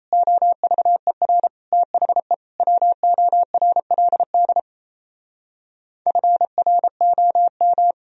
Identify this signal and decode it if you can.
Morse code, heard as a keyed tone, about 38 wpm, 700 Hz. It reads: OVER THE WORLD FROM